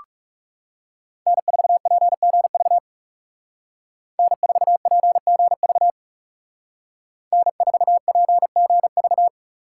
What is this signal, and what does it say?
Morse code, heard as a keyed tone, about 40 wpm, 700 Hz. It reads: N4PGV N4PGV N4PGV